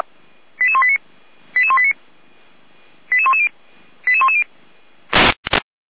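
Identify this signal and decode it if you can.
Five-tone selective call (ZVEI), heard as quick runs of five tones, tones of about 70 ms. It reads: 88189 88189 88110 88110
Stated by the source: ZVEI five-tone selective call received via SDR.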